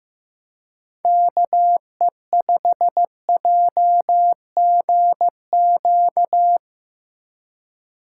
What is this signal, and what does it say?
Morse code, heard as a keyed tone, about 17 wpm, 700 Hz. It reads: KE5JGQ
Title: Speed Racing - USA 2x3 Call Signs 15wpm